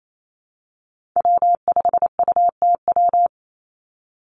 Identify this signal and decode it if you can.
Morse code, heard as a keyed tone, about 28 wpm, 700 Hz. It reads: W5UTW
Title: Call Signs - 28wpm